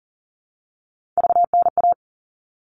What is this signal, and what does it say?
Morse code, heard as a keyed tone, about 40 wpm, 700 Hz. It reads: VNA
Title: All Cut Numbers - 3-Digit 40wpm